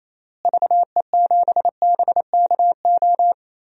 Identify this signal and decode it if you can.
Morse code, heard as a keyed tone, about 28 wpm, 700 Hz. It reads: VE7BKO
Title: DXCC-100 Call Signs - 6 Characters - Repeated 3x - 28wpm